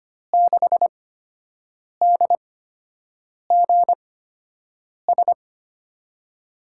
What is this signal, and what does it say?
Morse code, heard as a keyed tone, about 25 wpm, spaced as for 7 wpm, 700 Hz. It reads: 6DGS
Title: USA Call Signs - 5 Characters - 8x Character Spacing - 25wpm